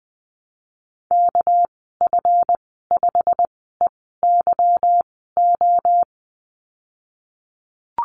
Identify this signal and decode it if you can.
Morse code, heard as a keyed tone, about 20 wpm, 700 Hz. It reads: K F 5 E Y O
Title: 238 Lesson 38g - Letter q - Call Signs 20wpm